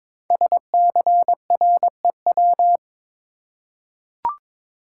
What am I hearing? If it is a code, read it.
Morse code, heard as a keyed tone, about 22 wpm, 700 Hz. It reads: SCREW
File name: 5 Letter Words - First 3 Letters Given - 22wpm